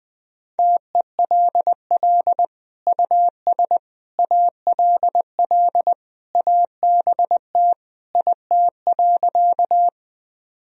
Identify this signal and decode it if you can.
Morse code, heard as a keyed tone, about 21 wpm, 700 Hz. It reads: TELL US ALL ABT IT.
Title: Sentences from Top 200 Words and Abbreviations 20wpm